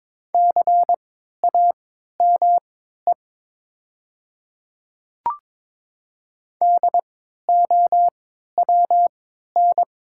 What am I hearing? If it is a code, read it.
Morse code, heard as a keyed tone, about 22 wpm, spaced as for 12 wpm, 700 Hz. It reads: CAME DOWN
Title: 3x Character Spacing - Top 200 Words - 22wpm